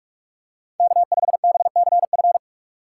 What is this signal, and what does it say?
Morse code, heard as a keyed tone, about 47 wpm, 700 Hz. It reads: K5BCF